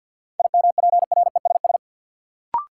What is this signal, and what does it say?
Morse code, heard as a keyed tone, about 49 wpm, 700 Hz. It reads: IMPRESS
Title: Sentences from Top 1600 Words - New Word Drill - 50wpm